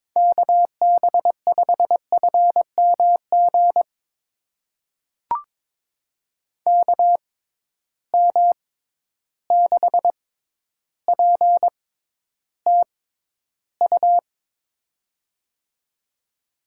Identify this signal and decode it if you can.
Morse code, heard as a keyed tone, about 22 wpm, spaced as for 11 wpm, 700 Hz. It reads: KB5FMG KM6PTU
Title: USA Call Signs - 6 Characters - 6x Character Spacing - 22wpm